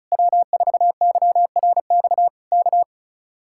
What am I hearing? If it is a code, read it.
Morse code, heard as a keyed tone, about 33 wpm, 700 Hz. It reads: W4YRX K
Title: Speed-Racing Common QSO Phrases Encore - 35wpm